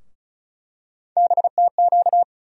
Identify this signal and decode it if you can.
Morse code, heard as a keyed tone, about 34 wpm, 700 Hz. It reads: BTQ
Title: USA Call Sign Postfixes - 3 Letters 35wpm